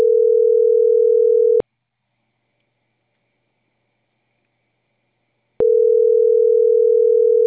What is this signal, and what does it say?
Ringback tone (North America), 440 + 480 Hz.